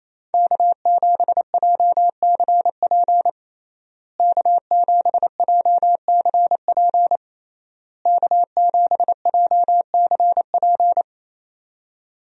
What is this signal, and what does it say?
Morse code, heard as a keyed tone, about 28 wpm, 700 Hz. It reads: K7JCP K7JCP K7JCP
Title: USA Call Signs - 5 Characters - Repeated 3x - 28wpm